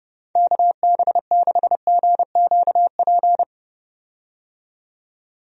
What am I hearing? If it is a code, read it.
Morse code, heard as a keyed tone, about 30 wpm, 700 Hz. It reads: KB6GQP
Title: USA Call Signs - 6 Characters - 30wpm